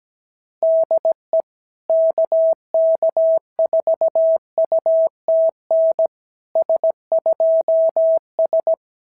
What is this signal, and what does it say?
Morse code, heard as a keyed tone, about 17 wpm, 690 Hz. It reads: DE KK4UTN S2S
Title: SOTA QSO Practice 17wpm